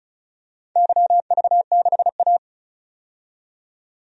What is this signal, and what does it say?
Morse code, heard as a keyed tone, about 36 wpm, 700 Hz. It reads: YV6A
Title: DXCC-100 Call Signs - 4 Characters - 35wpm